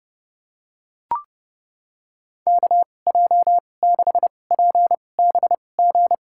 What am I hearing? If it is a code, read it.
Morse code, heard as a keyed tone, about 30 wpm, spaced as for 22 wpm, 700 Hz. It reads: KJ6PBG
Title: USA Call Signs - 2x Character Spacing - 30wpm